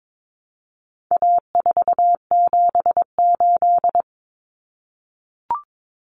Morse code, heard as a keyed tone, about 22 wpm, 700 Hz. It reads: A478
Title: Common Cut Numbers - 4-Digit 22wpm